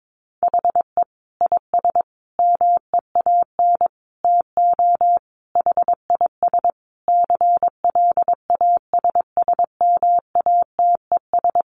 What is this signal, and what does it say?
Morse code, heard as a keyed tone, about 22 wpm, 700 Hz. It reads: HE IS MEAN TO HIS CLASSMATES
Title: Additional Easy Phrases 22wpm